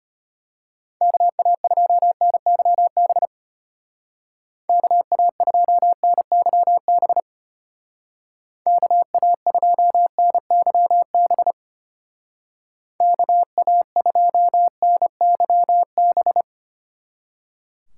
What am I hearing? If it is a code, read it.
Morse code, heard as a keyed tone, about 31 wpm, 700 Hz. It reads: KA2NYB KA2NYB KA2NYB KA2NYB